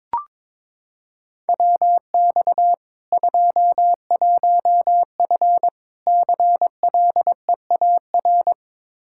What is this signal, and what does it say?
Morse code, heard as a keyed tone, about 22 wpm, 700 Hz. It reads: WX 21F CLEAR
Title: K1USN SST QSO Practice 22wpm